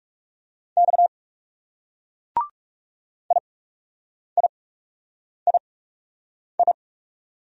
Morse code, heard as a keyed tone, about 40 wpm, spaced as for 8 wpm, 700 Hz. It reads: X IIII